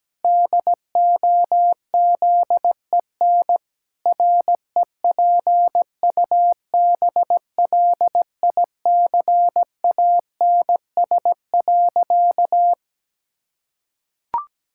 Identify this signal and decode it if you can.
Morse code, heard as a keyed tone, about 17 wpm, 700 Hz. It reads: DOZEN REPUBLICANS.
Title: Pangram Challenge v2 - 17wpm